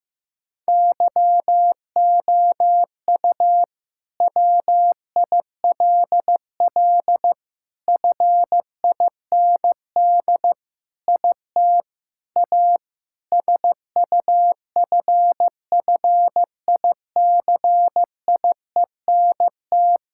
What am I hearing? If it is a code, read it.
Morse code, heard as a keyed tone, about 15 wpm, 700 Hz. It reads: YOU WILL FIND IT A SUFFICIENT
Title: Sentences from Top 1600 Words 15wpm